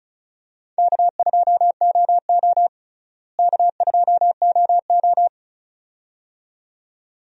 Morse code, heard as a keyed tone, about 36 wpm, 700 Hz. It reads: K2OO K2OO
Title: USA Call Signs - 4 Characters - Repeated 2x - 35wpm